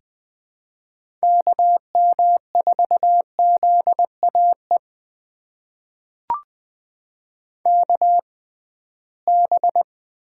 Morse code, heard as a keyed tone, about 20 wpm, spaced as for 15 wpm, 700 Hz. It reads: KM4ZAE KB
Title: USA Call Signs - 6 Characters - 6x Character Spacing - 20wpm